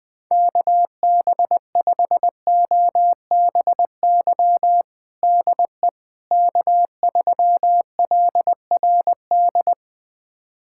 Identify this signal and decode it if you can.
Morse code, heard as a keyed tone, about 21 wpm, 700 Hz. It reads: KB5OBY DE K3LRD
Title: USA Callsign DE Callsign 20wpm